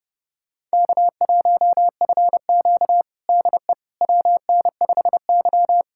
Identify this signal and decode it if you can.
Morse code, heard as a keyed tone, about 32 wpm, 700 Hz. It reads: K1FQ DE WN5Y
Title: USA Callsign DE Callsign - 4 Characters - 30wpm